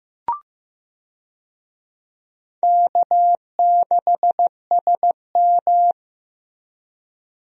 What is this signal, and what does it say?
Morse code, heard as a keyed tone, about 15 wpm, 700 Hz. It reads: K6SM
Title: USA Call Signs - 4 Characters - Repeated 2x - 15wpm